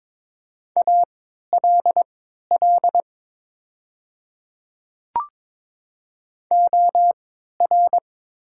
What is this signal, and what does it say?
Morse code, heard as a keyed tone, about 21 wpm, spaced as for 12 wpm, 700 Hz. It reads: ALL OR
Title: 3x Character Spacing - Top 100 Words - 22wpm